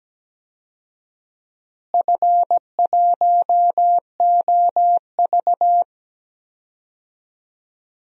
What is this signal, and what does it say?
Morse code, heard as a keyed tone, about 17 wpm, 700 Hz. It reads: F1OV
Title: DXCC-100 Call Signs - 4 Characters - Repeated 3x - 17wpm